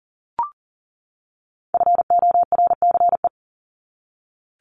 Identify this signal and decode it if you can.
Morse code, heard as a keyed tone, about 40 wpm, 700 Hz. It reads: FORCE
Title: ED Suffix - Top 100 Words with Base Words 40wpm